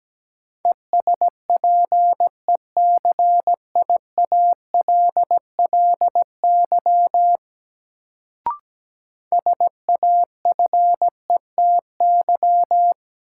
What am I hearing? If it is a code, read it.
Morse code, heard as a keyed tone, about 17 wpm, 700 Hz. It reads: ESPECIALLY SAFETY
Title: Sentences from Top 700 Words - New Word Drill - 17wpm